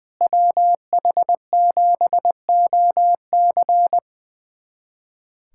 Morse code, heard as a keyed tone, about 21 wpm, 700 Hz. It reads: WH7OC
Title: USA Callsign DE Callsign - 5 Characters - 20wpm